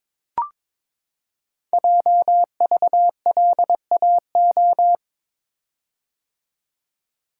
Morse code, heard as a keyed tone, about 22 wpm, 700 Hz. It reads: JVLAO
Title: Code Groups - 5 Letter:Numbers - 22wpm